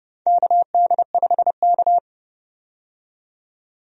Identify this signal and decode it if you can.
Morse code, heard as a keyed tone, about 32 wpm, 700 Hz. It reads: KD5K
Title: USA Call Signs - 4 Characters - 30wpm